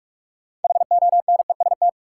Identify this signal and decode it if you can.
Morse code, heard as a keyed tone, about 46 wpm, 700 Hz. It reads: HONEST
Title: Sets of 5 Words Encore - New Word Drill - 45wpm